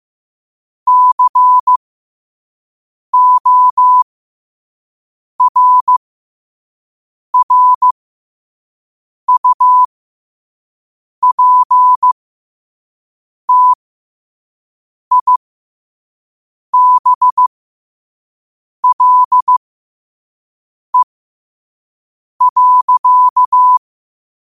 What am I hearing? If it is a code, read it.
Morse code, heard as a keyed tone, about 15 wpm, spaced as for 5 wpm, 1000 Hz. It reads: CORRUPTIBLE.